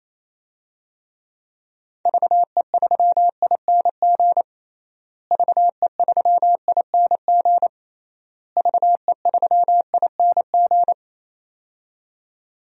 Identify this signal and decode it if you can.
Morse code, heard as a keyed tone, about 28 wpm, 700 Hz. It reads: VE3ING VE3ING VE3ING
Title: DXCC-100 Call Signs - 6 Characters - Repeated 3x - 28wpm